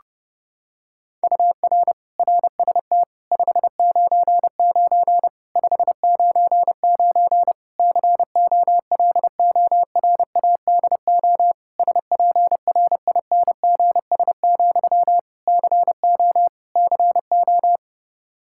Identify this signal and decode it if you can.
Morse code, heard as a keyed tone, about 30 wpm, 700 Hz. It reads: UR RST 599 599 COLORADO SPRINGS, CO CO
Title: K1USN SST QSO Practice 30wpm